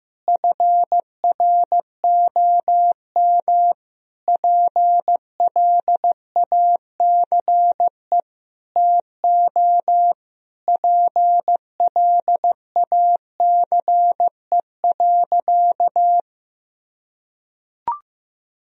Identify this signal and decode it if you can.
Morse code, heard as a keyed tone, about 15 wpm, 700 Hz. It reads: FROM PLACE TO PLACE.
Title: Sentences from Top 1000 Words - Review Part 1 - 15wpm